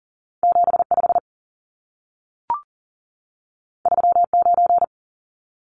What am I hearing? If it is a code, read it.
Morse code, heard as a keyed tone, about 40 wpm, 700 Hz. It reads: 75 39
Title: Numbers - 2-Digit 40wpm